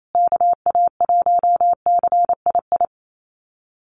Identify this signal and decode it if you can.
Morse code, heard as a keyed tone, about 28 wpm, 700 Hz. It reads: KA1CII
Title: USA Call Signs Repeated 3x 28wpm